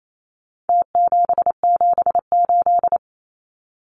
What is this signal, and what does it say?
Morse code, heard as a keyed tone, about 28 wpm, 700 Hz. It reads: T778
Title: Numbers - Contest Style - 4-Digit 28wpm